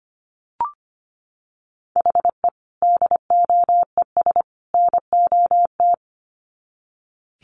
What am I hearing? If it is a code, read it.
Morse code, heard as a keyed tone, about 25 wpm, 700 Hz. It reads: HE DOES NOT
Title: Sets of 3 Words 25wpm